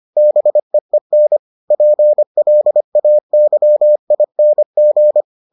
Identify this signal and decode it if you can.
Morse code, heard as a keyed tone, about 25 wpm, 600 Hz. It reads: BEEN PLAYING